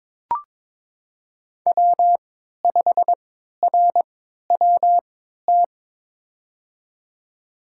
Morse code, heard as a keyed tone, about 22 wpm, spaced as for 13 wpm, 700 Hz. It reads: W5RWT
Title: USA Call Signs - 5 Characters - 3x Character Spacing - 22wpm